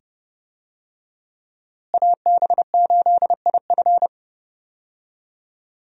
Morse code, heard as a keyed tone, about 32 wpm, 700 Hz. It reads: AB8IF
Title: USA Call Signs - 5 Characters - Repeated 2x - 30wpm